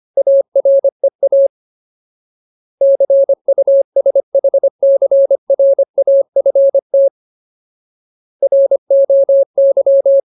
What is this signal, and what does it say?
Morse code, heard as a keyed tone, about 25 wpm, 550 Hz. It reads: AREA CUSHCRAFT ROY
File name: Ultimate Word List / 25WPM Ultimate